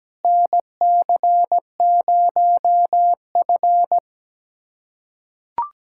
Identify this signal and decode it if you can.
Morse code, heard as a keyed tone, about 17 wpm, 700 Hz. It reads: NC0F
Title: USA Callsign DE Callsign - 4 Characters - 17wpm